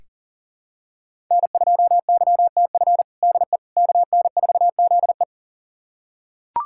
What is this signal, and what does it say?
Morse code, heard as a keyed tone, about 40 wpm, 700 Hz. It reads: N2YTF DE KN4ZE
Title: USA Callsign DE Callsign - 5 Characters - 40wpm